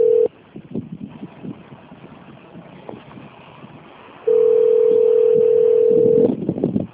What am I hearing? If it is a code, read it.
Ringback tone (North America), 440 + 480 Hz.
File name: out- - - - - .31662